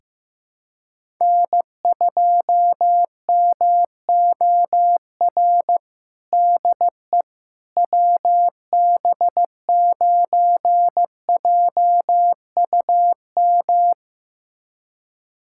Morse code, heard as a keyed tone, about 16 wpm, 700 Hz. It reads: N2MOR DE WB9JUM
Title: USA Callsign DE Callsign 15wpm